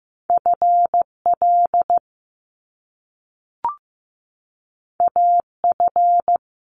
Morse code, heard as a keyed tone, about 15 wpm, 700 Hz. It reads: FL AF
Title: Top Two Letter Combinations in Words 15wpm